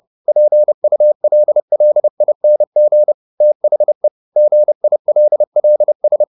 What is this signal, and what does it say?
Morse code, heard as a keyed tone, about 30 wpm, 600 Hz. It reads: PULLING THE GILLS